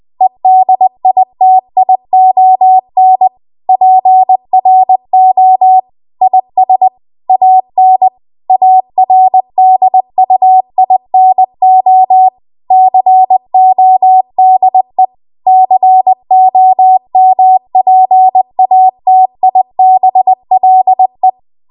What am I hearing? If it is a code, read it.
Morse code, heard as a keyed tone, about 20 wpm, 750 Hz. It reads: EDITION PRO IS AN ARDUINO CODE COMPATIBLE